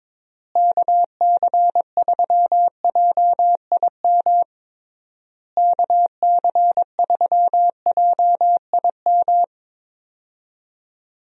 Morse code, heard as a keyed tone, about 22 wpm, 700 Hz. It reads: KC3JIM KC3JIM
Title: USA Call Signs - 6 Characters - Repeated 3x - 22wpm